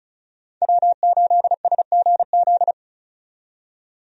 Morse code, heard as a keyed tone, about 34 wpm, 700 Hz. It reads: W8SGZ
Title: USA Call Signs - 5 Characters - 35wpm